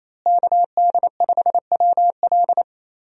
Morse code, heard as a keyed tone, about 28 wpm, 700 Hz. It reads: KD5WL
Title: USA Call Signs - 5 Characters - Repeated 2x - 28wpm